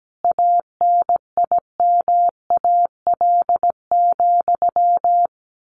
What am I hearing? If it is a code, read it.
Morse code, heard as a keyed tone, about 17 wpm, 700 Hz. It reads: ANIMAL,
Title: A Sound Of Thunder 17wpm